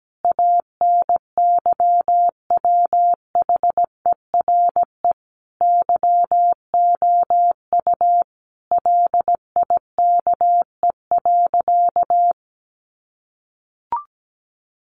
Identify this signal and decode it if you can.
Morse code, heard as a keyed tone, about 17 wpm, 700 Hz. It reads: ANYWHERE YOU LIKE.
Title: Sentences from Top 1400 Words 17wpm